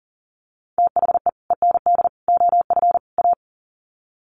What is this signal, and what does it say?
Morse code, heard as a keyed tone, about 39 wpm, 700 Hz. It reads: THE END OF A
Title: Sets of 4 Words 40wpm